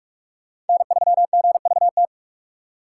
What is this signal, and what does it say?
Morse code, heard as a keyed tone, about 45 wpm, 700 Hz. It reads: N3GVT
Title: USA Call Signs - 5 Characters - 3x Character Spacing - 45wpm